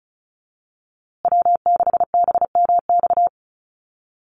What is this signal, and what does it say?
Morse code, heard as a keyed tone, about 34 wpm, 700 Hz. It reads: W6BMX
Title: USA Call Signs Repeated 3x 35wpm